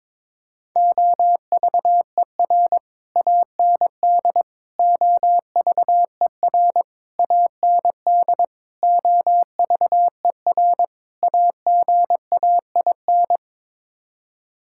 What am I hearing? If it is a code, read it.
Morse code, heard as a keyed tone, about 22 wpm, 700 Hz. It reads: OVER AND OVER AND OVER AGAIN
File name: practice / 22wpm / Sets of 6 Words - 22wpm